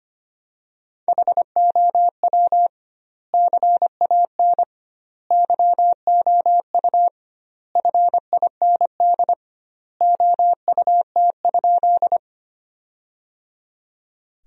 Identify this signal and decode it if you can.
Morse code, heard as a keyed tone, about 25 wpm, 700 Hz. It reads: HOW CAN YOU FIND OUT?
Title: 2x Word-Spacing - Sentences from Top 200 Words - 25wpm